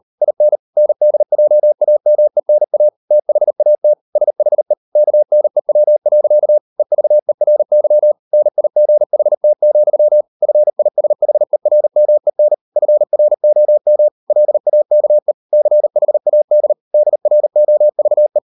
Morse code, heard as a keyed tone, about 38 wpm, 600 Hz. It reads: IN NDJAMENA THAT SHE KNEW. EVERY NIGHT, FISHERMEN FROM LAKE CHAD DROVE